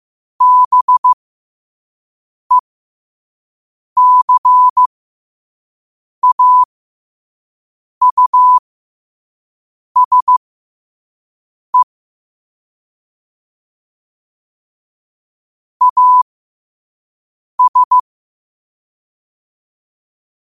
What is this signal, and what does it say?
Morse code, heard as a keyed tone, about 15 wpm, spaced as for 5 wpm, 1000 Hz. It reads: BECAUSE AS
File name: qotdfiles / 5wpm